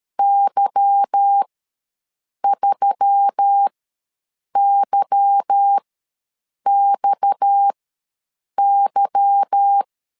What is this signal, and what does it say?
Morse code, heard as a keyed tone, about 13 wpm, spaced as for 7 wpm, 800 Hz. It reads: Y3YXY
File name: LEZIONI / 8 YJ45 / 8 Y ws 12wpm 3409